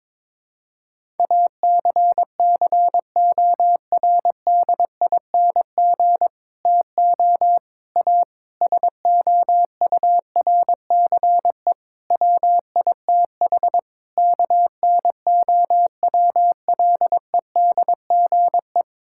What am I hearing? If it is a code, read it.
Morse code, heard as a keyed tone, about 21 wpm, 700 Hz. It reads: ACCORDING TO A SOURCE WITH KNOWLEDGE
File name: Sentences with 9-Word Phrases - 22wpm